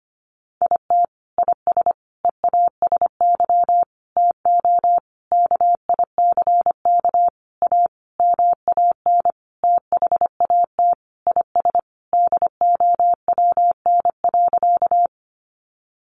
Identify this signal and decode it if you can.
Morse code, heard as a keyed tone, about 25 wpm, 700 Hz. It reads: IT IS EASY TO KICK A MAN THAT IS DOWN.
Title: Sentences from Top 1300 Words 25wpm